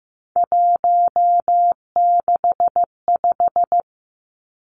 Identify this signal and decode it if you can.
Morse code, heard as a keyed tone, about 15 wpm, 700 Hz. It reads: 165
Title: Three Challenging Letter-Number Combinations 15wpm